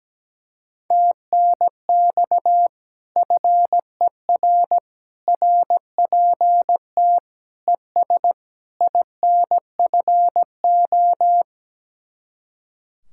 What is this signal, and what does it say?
Morse code, heard as a keyed tone, about 17 wpm, 700 Hz. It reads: TNX FER RPT ES INFO
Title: Speed-Racing Common QSO Phrases - 17wpm